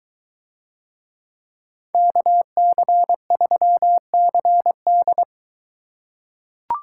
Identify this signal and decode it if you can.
Morse code, heard as a keyed tone, about 23 wpm, 700 Hz. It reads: KC3CD